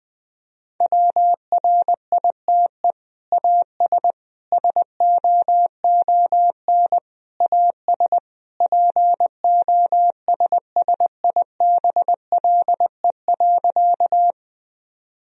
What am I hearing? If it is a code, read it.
Morse code, heard as a keyed tone, about 20 wpm, 700 Hz. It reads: WRITE AS SOON AS POSSIBLE.
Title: Sentences from Top 1000 Words - Review Part 1 - 20wpm